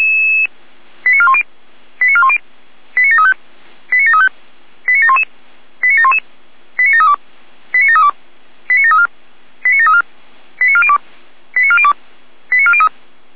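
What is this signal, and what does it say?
Five-tone selective call (ZVEI), heard as quick runs of five tones, tones of about 70 ms. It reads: 89410 89410 89735 89735 89711 89711 89732 89732 89734 89734 89402 89402 89403
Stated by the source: ZVEI five-tone call received on SDR.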